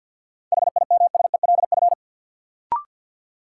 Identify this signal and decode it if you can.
Morse code, heard as a keyed tone, about 51 wpm, 700 Hz. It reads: HIMSELF
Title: Sentences from Top 1500 Words - New Word Drill Alt - 50wpm